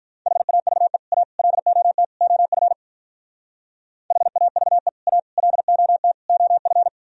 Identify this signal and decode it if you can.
Morse code, heard as a keyed tone, about 44 wpm, 700 Hz. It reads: HAVE A LOT OF HAVE A LOT OF